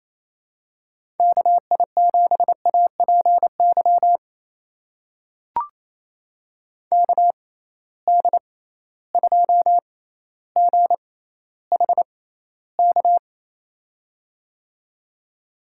Morse code, heard as a keyed tone, about 27 wpm, spaced as for 14 wpm, 700 Hz. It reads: KI7APY KD2GHK
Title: USA Call Signs - 6 Characters - 6x Character Spacing - 28wpm